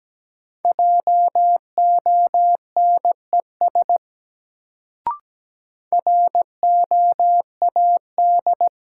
Morse code, heard as a keyed tone, about 17 wpm, 700 Hz. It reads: JONES ROAD
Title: Sets of 5 Words - New Word Drill - 17wpm